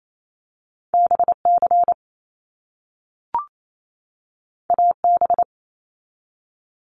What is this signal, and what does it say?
Morse code, heard as a keyed tone, about 28 wpm, 700 Hz. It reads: BC AB
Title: Canadian Provinces and Territories 28wpm